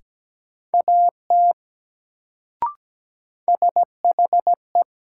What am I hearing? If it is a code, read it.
Morse code, heard as a keyed tone, about 17 wpm, 700 Hz. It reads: AT SHE
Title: Sentences from Top 100 Words - New Word Drill - 17wpm